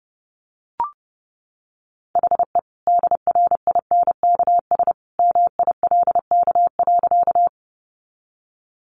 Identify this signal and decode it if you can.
Morse code, heard as a keyed tone, about 30 wpm, 700 Hz. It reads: HE DRINKS MILK.
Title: Easy Phrases 30wpm